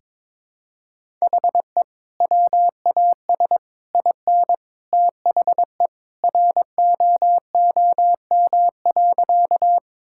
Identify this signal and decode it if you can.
Morse code, heard as a keyed tone, about 22 wpm, 700 Hz. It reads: HE WAS IN THE ROOM.
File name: practice / 22wpm / Sentences with 2-Word Phrases - 22wpm